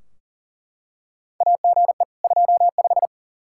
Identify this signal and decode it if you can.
Morse code, heard as a keyed tone, about 40 wpm, 700 Hz. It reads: AGE 25
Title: Practice QSO - Exchange 3 - More Info - 40wpm